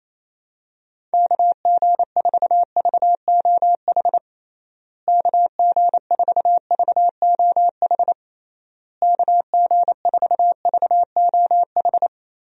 Morse code, heard as a keyed tone, about 28 wpm, 700 Hz. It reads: KG4VOH KG4VOH KG4VOH
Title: USA Call Signs - 6 Characters - Repeated 3x - 28wpm